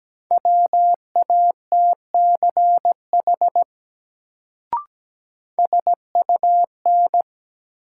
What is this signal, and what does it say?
Morse code, heard as a keyed tone, about 17 wpm, 700 Hz. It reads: WATCH SUN
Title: Sentences from Top 300 Words - New Word Drill - 17wpm